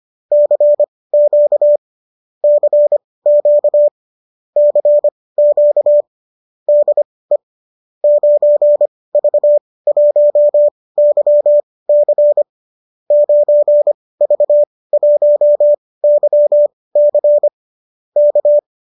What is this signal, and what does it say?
Morse code, heard as a keyed tone, about 25 wpm, spaced as for 18 wpm, 590 Hz. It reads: CQ CQ CQ DE 9V1YC 9V1YC K